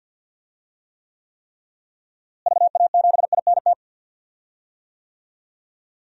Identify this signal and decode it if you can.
Morse code, heard as a keyed tone, about 49 wpm, 700 Hz. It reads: VA7INT
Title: DXCC-100 Call Signs - 6 Characters - Repeated 3x - 50wpm